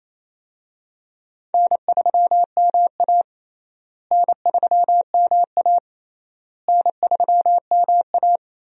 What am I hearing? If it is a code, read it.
Morse code, heard as a keyed tone, about 28 wpm, 700 Hz. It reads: N3MA N3MA N3MA
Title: USA Call Signs - 4 Characters - Repeated 3x - 28wpm